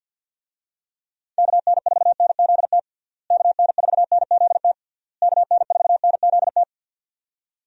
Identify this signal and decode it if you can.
Morse code, heard as a keyed tone, about 52 wpm, 700 Hz. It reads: KN4NZT KN4NZT KN4NZT
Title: USA Call Signs - 6 Characters - Repeated 3x - 50wpm